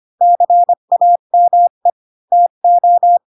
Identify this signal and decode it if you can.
Morse code, heard as a keyed tone, about 25 wpm, 700 Hz. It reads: CAME TO